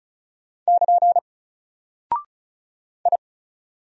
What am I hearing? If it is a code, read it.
Morse code, heard as a keyed tone, about 35 wpm, 700 Hz. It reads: <KN> I
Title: Single Character (Letter:Number:Punctuation:Prosign) 35wpm